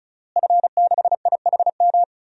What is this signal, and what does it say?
Morse code, heard as a keyed tone, about 36 wpm, 700 Hz. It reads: F6IHM
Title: DXCC-100 Call Signs - 5 Characters - Repeated 3x - 35wpm